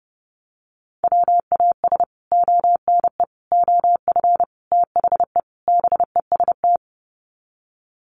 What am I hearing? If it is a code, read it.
Morse code, heard as a keyed tone, about 30 wpm, 700 Hz. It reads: WAS ONE OF THE BEST
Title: Sets of 5 Words Encore2 30wpm